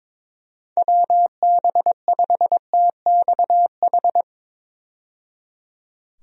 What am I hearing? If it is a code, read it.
Morse code, heard as a keyed tone, about 22 wpm, 700 Hz. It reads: WB5TXH
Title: USA Call Signs - 6 Characters - 22wpm